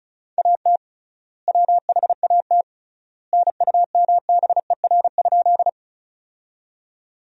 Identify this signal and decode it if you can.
Morse code, heard as a keyed tone, about 36 wpm, 700 Hz. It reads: AT WHAT NUMBER?
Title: 3x Word-Spacing - Sentences from Top 100 Words - 35wpm